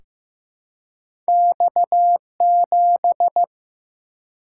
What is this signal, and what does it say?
Morse code, heard as a keyed tone, about 15 wpm, 700 Hz. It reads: X7
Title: Single-Letter Single-Number 15wpm